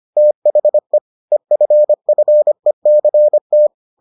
Morse code, heard as a keyed tone, about 25 wpm, 600 Hz. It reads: THE EFFECT